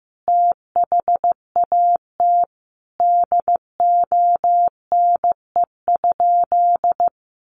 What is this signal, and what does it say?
Morse code, heard as a keyed tone, about 15 wpm, 700 Hz. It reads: THAT DONE?
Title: Sentences from Top 1100 Words 15wpm